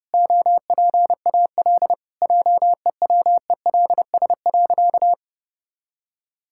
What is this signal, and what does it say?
Morse code, heard as a keyed tone, about 30 wpm, 700 Hz. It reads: OPAL JEWELS.
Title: Pangram Challenge v2 - 30wpm